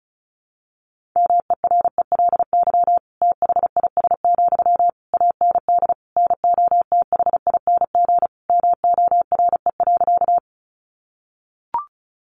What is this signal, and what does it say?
Morse code, heard as a keyed tone, about 34 wpm, 700 Hz. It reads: MERELY THIS, AND NOTHING MORE.
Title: Sentences from Top 1400 Words 35wpm